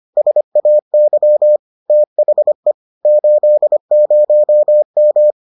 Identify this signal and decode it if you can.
Morse code, heard as a keyed tone, about 25 wpm, 600 Hz. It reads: SAY THE 80M